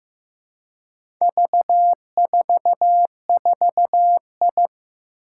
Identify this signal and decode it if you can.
Morse code, heard as a keyed tone, about 15 wpm, 700 Hz. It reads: V44I
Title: Four Challenging Letter-Number Combinations 15wpm